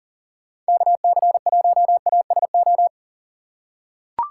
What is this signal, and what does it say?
Morse code, heard as a keyed tone, about 40 wpm, 700 Hz. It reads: KC1ASO
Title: USA Call Signs - 6 Characters - 6x Character Spacing - 40wpm